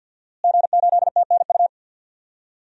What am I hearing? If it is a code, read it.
Morse code, heard as a keyed tone, about 52 wpm, 700 Hz. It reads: G8TNU
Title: DXCC-100 Call Signs - 5 Characters - Repeated 3x - 50wpm